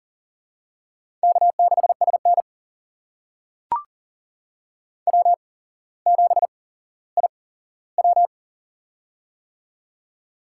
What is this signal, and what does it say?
Morse code, heard as a keyed tone, about 40 wpm, spaced as for 17 wpm, 700 Hz. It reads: K6SN W7IW
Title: USA Call Signs - 4 Characters - 8x Character Spacing - 40wpm